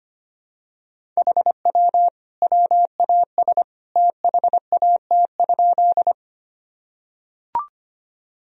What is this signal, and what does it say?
Morse code, heard as a keyed tone, about 25 wpm, 700 Hz. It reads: HW WAS THAT?
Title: Sentences from Top 100 Words and Abbreviations 25wpm